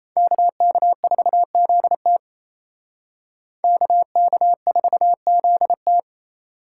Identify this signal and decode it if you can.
Morse code, heard as a keyed tone, about 35 wpm, 700 Hz. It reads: KK4ZT KK4ZT
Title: Speed Racing - USA Call Signs 28wpm